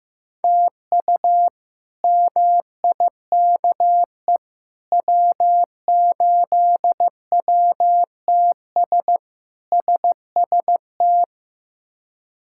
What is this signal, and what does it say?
Morse code, heard as a keyed tone, about 15 wpm, 700 Hz. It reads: TU MIKE W8WTS SST
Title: K1USN SST QSO Practice - 15wpm